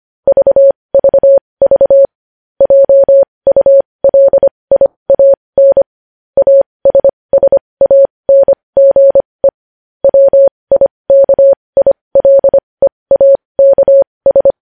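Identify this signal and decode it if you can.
Morse code, heard as a keyed tone, about 25 wpm, spaced as for 20 wpm, 550 Hz. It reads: VVV JULIAN ASSANGE WIKILEAKS
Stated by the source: Morse practice sending news headlines at 20 wpm.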